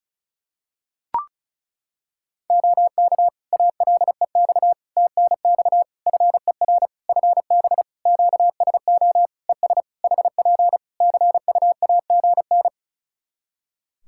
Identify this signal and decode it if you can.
Morse code, heard as a keyed tone, about 36 wpm, 700 Hz. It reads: OK ALEX TNX FER FB QSO ES HP CUAGN
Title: Practice QSO - Exchange 4 - Ending - 35wpm